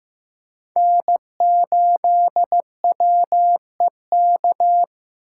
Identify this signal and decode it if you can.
Morse code, heard as a keyed tone, about 15 wpm, 700 Hz. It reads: N8WEK
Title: USA Call Signs - 5 Characters - 8x Character Spacing - 15wpm